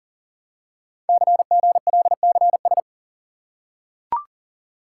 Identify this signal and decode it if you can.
Morse code, heard as a keyed tone, about 38 wpm, 700 Hz. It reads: CGLCS
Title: Code Groups - 5 Letters - 40wpm